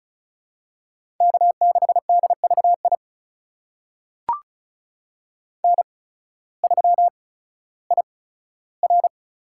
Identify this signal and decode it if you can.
Morse code, heard as a keyed tone, about 34 wpm, spaced as for 16 wpm, 690 Hz. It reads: K6DVI N3IR
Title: USA Call Signs - 5 Characters - 8x Character Spacing - 35wpm